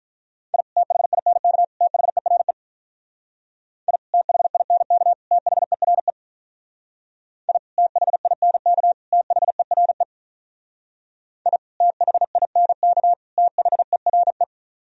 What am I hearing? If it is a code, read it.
Morse code, heard as a keyed tone, about 45 wpm, 700 Hz. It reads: I THINK THERE I THINK THERE I THINK THERE I THINK THERE